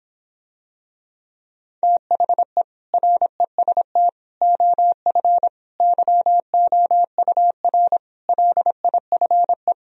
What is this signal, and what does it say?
Morse code, heard as a keyed tone, about 26 wpm, 700 Hz. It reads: THE REST OF YOUR LIFE